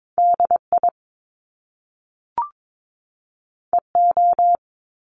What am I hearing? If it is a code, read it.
Morse code, heard as a keyed tone, about 22 wpm, 700 Hz. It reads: DI EO
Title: Top Two Letter Combinations in Words 22wpm